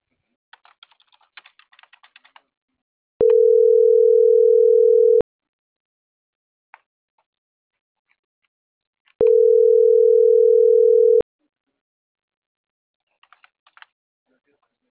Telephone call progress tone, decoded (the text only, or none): ringback tone